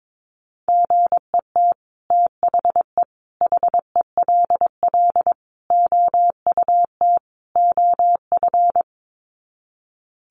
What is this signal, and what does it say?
Morse code, heard as a keyed tone, about 22 wpm, 700 Hz. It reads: GET THE HELL OUT OF
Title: Sets of 5 Words Encore2 22wpm